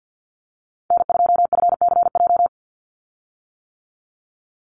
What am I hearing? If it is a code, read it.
Morse code, heard as a keyed tone, about 50 wpm, 700 Hz. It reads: N2FCJ
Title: USA Call Signs 50wpm